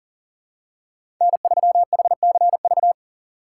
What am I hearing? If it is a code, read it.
Morse code, heard as a keyed tone, about 40 wpm, 700 Hz. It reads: N3HCV
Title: USA Call Signs - 5 Characters - 3x Character Spacing - 40wpm